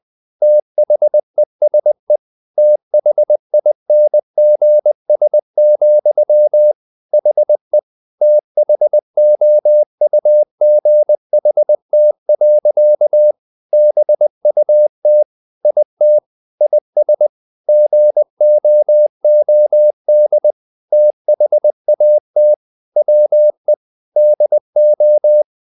Morse code, heard as a keyed tone, about 20 wpm, 600 Hz. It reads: THESE THINGS, HE THOUGHT. BUT IT IS GOOD THAT WE DO
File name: the old man and the sea part 2 20wpm